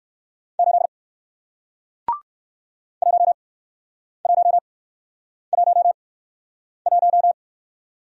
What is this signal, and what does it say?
Morse code, heard as a keyed tone, about 57 wpm, spaced as for 9 wpm, 700 Hz. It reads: 8 1111